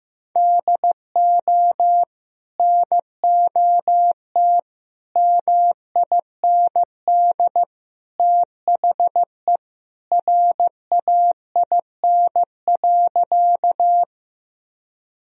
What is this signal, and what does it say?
Morse code, heard as a keyed tone, about 15 wpm, 700 Hz. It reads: DO NOT MIND THE RAIN.
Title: Sentences from Top 1000 Words - Review Part 1 - 15wpm